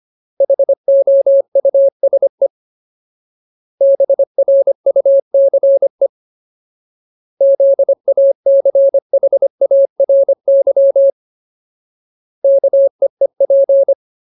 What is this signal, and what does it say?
Morse code, heard as a keyed tone, about 25 wpm, 550 Hz. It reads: HOUSE BRUCE ZACHARY KEEP
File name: Ultimate Word List / 25WPM Ultimate